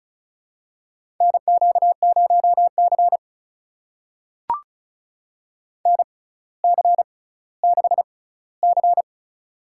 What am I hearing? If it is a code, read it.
Morse code, heard as a keyed tone, about 36 wpm, spaced as for 18 wpm, 700 Hz. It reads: NQ0C NC6C
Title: USA Call Signs - 4 Characters - 6x Character Spacing - 35wpm